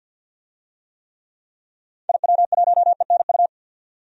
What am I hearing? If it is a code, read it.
Morse code, heard as a keyed tone, about 51 wpm, 700 Hz. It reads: IW1ENU
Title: DXCC-100 Call Signs - 6 Characters - 50wpm